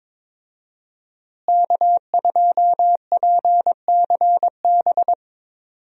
Morse code, heard as a keyed tone, about 22 wpm, 700 Hz. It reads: K2PCB
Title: USA Call Signs - 5 Characters - Repeated 2x - 22wpm